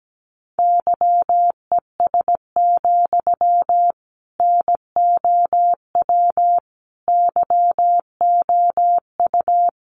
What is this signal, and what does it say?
Morse code, heard as a keyed tone, about 17 wpm, 700 Hz. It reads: YES, NOW YOU
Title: Sentences from Top 800 Words 17wpm